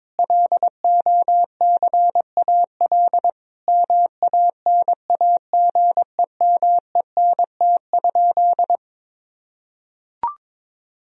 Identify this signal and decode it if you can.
Morse code, heard as a keyed tone, about 22 wpm, 700 Hz. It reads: LOCAL MANAGEMENT?
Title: Sentences from Top 1000 Words - Review Part 4 - 22wpm